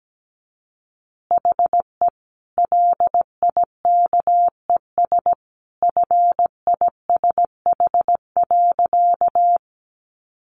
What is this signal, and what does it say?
Morse code, heard as a keyed tone, about 17 wpm, 700 Hz. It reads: HE LIKES FISH.
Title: Easy Phrases 17wpm